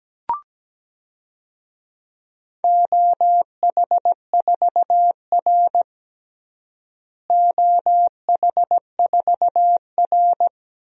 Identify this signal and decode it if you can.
Morse code, heard as a keyed tone, about 17 wpm, 700 Hz. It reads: OH4R OH4R
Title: DXCC-100 Call Signs - 4 Characters - Repeated 3x - 17wpm